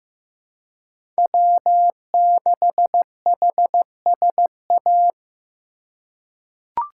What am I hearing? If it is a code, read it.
Morse code, heard as a keyed tone, about 16 wpm, 700 Hz. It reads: W6HSA
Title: USA Call Signs - 5 Characters - 4x Character Spacing - 15wpm